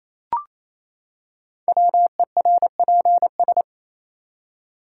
Morse code, heard as a keyed tone, about 27 wpm, 700 Hz. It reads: WERPS
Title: Code Groups - 5 Letters - 28wpm